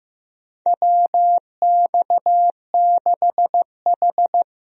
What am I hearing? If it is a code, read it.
Morse code, heard as a keyed tone, about 15 wpm, 700 Hz. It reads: WX6H
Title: USA Call Signs - 4 Characters - Repeated 2x - 15wpm